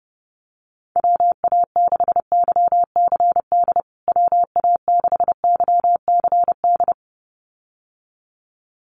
Morse code, heard as a keyed tone, about 30 wpm, 700 Hz. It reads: WA6YCD WA6YCD
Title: USA Call Signs Repeated 2x 30wpm